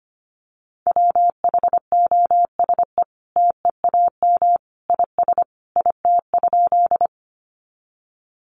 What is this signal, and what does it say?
Morse code, heard as a keyed tone, about 25 wpm, 700 Hz. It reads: WHOSE TEAM IS IT?
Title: Sentences from Top 1500 Words 25wpm